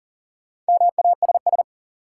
Morse code, heard as a keyed tone, about 40 wpm, 700 Hz. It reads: MASS
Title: Sets of 5 Words Encore - New Word Drill - 40wpm